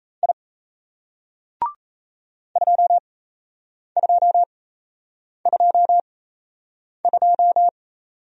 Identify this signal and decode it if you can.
Morse code, heard as a keyed tone, about 38 wpm, spaced as for 8 wpm, 700 Hz. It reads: I 2222